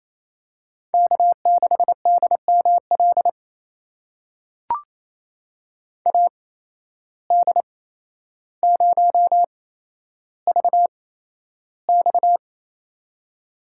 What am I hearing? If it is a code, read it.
Morse code, heard as a keyed tone, about 28 wpm, spaced as for 12 wpm, 700 Hz. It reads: K6DML AD0VX
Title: USA Call Signs - 5 Characters - 8x Character Spacing - 28wpm